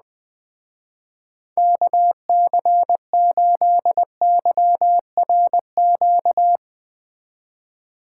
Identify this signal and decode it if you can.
Morse code, heard as a keyed tone, about 20 wpm, 700 Hz. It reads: KC8YRQ
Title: Speed Racing - USA 2x3 Call Signs 17wpm